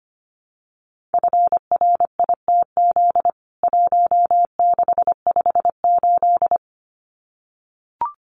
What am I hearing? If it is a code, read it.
Morse code, heard as a keyed tone, about 25 wpm, 700 Hz. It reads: FRITZ 1658
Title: CWT Exchange 25wpm